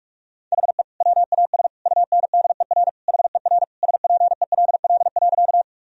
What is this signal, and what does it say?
Morse code, heard as a keyed tone, about 41 wpm, 700 Hz. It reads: HE WAS UNDER HER SPELL.
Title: Sentences from Top 1000 Words - Review Part 1 - 45wpm